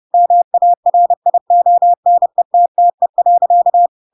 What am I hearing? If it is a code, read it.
Morse code, heard as a keyed tone, about 30 wpm, 700 Hz. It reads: MARIONETTE.
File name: Pinocchio / Pinocchio ch3 30 30 ik1hge